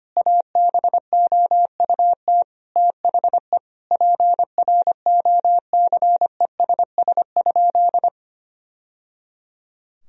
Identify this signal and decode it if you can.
Morse code, heard as a keyed tone, about 25 wpm, 700 Hz. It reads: ABOUT THE PROCESS?
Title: Sentences from Top 1000 Words - Review Part 3 - 25wpm